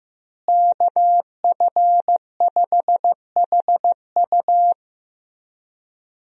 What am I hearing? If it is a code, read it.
Morse code, heard as a keyed tone, about 15 wpm, 700 Hz. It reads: KF5HU
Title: USA Call Signs - 5 Characters - Repeated 2x - 15wpm